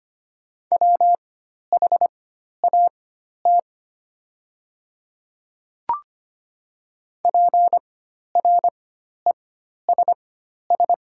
Morse code, heard as a keyed tone, about 25 wpm, spaced as for 12 wpm, 690 Hz. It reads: WHAT PRESS
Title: 4x Character Spacing - Top 300 Words - 25wpm